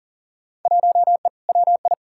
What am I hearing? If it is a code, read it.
Morse code, heard as a keyed tone, about 40 wpm, 700 Hz. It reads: 1E WI
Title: Field Day Exchanges - 40wpm